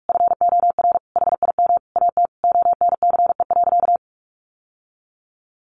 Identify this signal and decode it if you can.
Morse code, heard as a keyed tone, about 44 wpm, 700 Hz. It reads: FOR HIM AT ONCE.
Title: Sentences from Top 1500 Words 45wpm